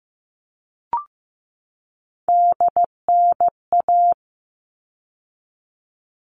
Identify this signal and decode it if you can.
Morse code, heard as a keyed tone, about 15 wpm, 700 Hz. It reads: DNA
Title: All Cut Numbers - 3-Digit 15wpm